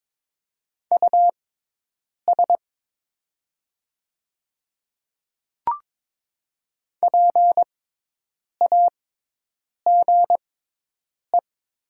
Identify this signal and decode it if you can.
Morse code, heard as a keyed tone, about 22 wpm, spaced as for 8 wpm, 700 Hz. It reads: US PAGE
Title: 6x Character Spacing - Top 500 Words - 22wpm